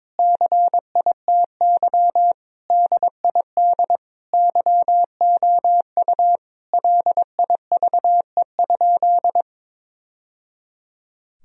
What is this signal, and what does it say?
Morse code, heard as a keyed tone, about 22 wpm, 700 Hz. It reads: CITY DID YOU LIVE?
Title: Sentences from Top 1000 Words - Review Part 1 - 22wpm